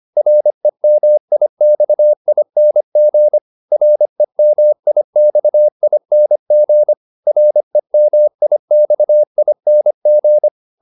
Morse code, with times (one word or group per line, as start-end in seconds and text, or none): 0.2-3.4 REMIXING
3.7-6.9 REMIXING
7.3-10.5 REMIXING